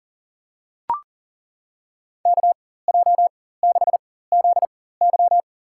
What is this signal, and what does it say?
Morse code, heard as a keyed tone, about 39 wpm, spaced as for 19 wpm, 700 Hz. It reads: KJ6ZY
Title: USA Call Signs - 5 Characters - 4x Character Spacing - 40wpm